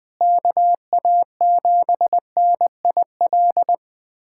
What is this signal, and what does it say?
Morse code, heard as a keyed tone, about 20 wpm, 700 Hz. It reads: KA7NIL
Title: USA Callsign DE Callsign 20wpm